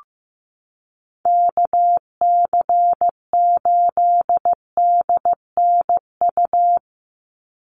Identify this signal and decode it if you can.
Morse code, heard as a keyed tone, about 15 wpm, 700 Hz. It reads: KC8DNU
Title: USA Call Signs 15wpm